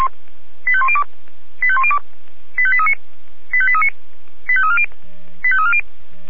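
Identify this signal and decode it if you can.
Five-tone selective call (ZVEI), heard as quick runs of five tones, tones of about 70 ms. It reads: 85102 85102 85839 85839 85340 85340
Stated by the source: ZVEI five-tone selective call received on SDR.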